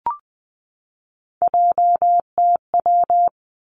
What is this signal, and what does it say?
Morse code, heard as a keyed tone, about 20 wpm, 700 Hz. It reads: JTW
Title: Three Challenging Letter-Number Combinations 20wpm